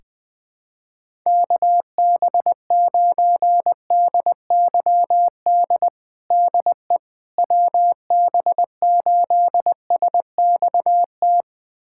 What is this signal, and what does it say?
Morse code, heard as a keyed tone, about 20 wpm, 700 Hz. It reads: KB9DYD DE WB8SXT
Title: USA Callsign DE Callsign - 6 Characters - 20wpm